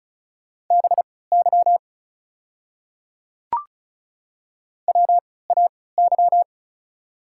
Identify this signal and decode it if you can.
Morse code, heard as a keyed tone, about 36 wpm, spaced as for 20 wpm, 700 Hz. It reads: BY WAY
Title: 3x Character Spacing - Top 200 Words - 35wpm